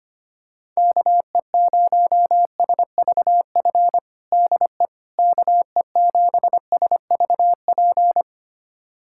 Morse code, heard as a keyed tone, about 25 wpm, 700 Hz. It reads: KE0SVF DE KE7SVP
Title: USA Callsign DE Callsign - 6 Characters - 25wpm